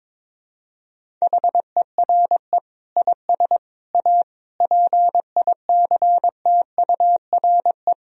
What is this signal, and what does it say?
Morse code, heard as a keyed tone, about 22 wpm, 700 Hz. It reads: HERE IS A PICTURE
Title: Sentences from Top 1000 Words - Review Part 1 - 22wpm